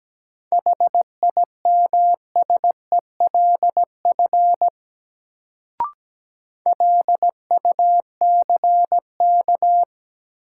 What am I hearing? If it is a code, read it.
Morse code, heard as a keyed tone, about 17 wpm, 700 Hz. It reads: HIMSELF LUCK
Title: Sentences from Top 1500 Words - New Word Drill Alt - 17wpm